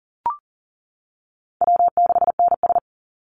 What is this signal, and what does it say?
Morse code, heard as a keyed tone, about 40 wpm, 700 Hz. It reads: W6NS
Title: Easy Call Signs - 40wpm